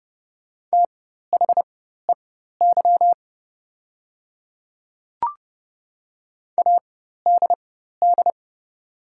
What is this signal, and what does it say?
Morse code, heard as a keyed tone, about 31 wpm, spaced as for 14 wpm, 690 Hz. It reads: THEY ADD
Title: 4x Character Spacing - Top 500 Words - 30wpm